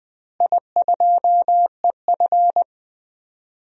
Morse code, heard as a keyed tone, about 21 wpm, 700 Hz. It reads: I2EF
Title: DXCC-100 Call Signs - 4 Characters - Repeated 3x - 20wpm